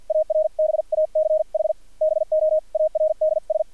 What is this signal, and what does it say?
Morse code, heard as a keyed tone, about 50 wpm, spaced as for 42 wpm, 620 Hz, with radio noise. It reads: ARBACH BOARDS
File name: archived / CRSnet